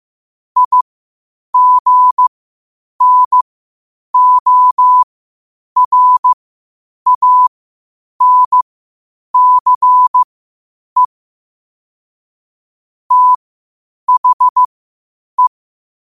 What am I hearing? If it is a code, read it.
Morse code, heard as a keyed tone, about 15 wpm, spaced as for 8 wpm, 1000 Hz. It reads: IGNORANCE THE
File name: qotdfiles / 8wpm